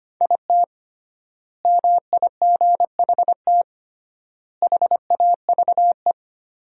Morse code, heard as a keyed tone, about 25 wpm, 700 Hz. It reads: IT MIGHT HAVE
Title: 3x Word-Spacing - Sentences from Top 500 Words - 25wpm